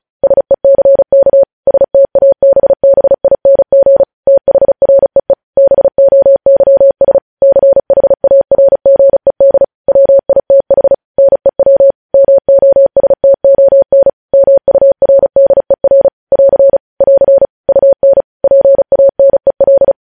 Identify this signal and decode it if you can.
Morse code, heard as a keyed tone, about 36 wpm, 550 Hz. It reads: SECK STABBING THREE BOYS CHARGED WITH NEW MOSTON MURDER <AR> <AR> UN PANEL
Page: news headlines in morse code at 35 wpm